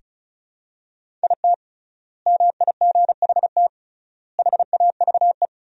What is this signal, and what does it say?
Morse code, heard as a keyed tone, about 34 wpm, 700 Hz. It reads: IT MIGHT HAVE
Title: 3x Word-Spacing - Sentences from Top 500 Words - 35wpm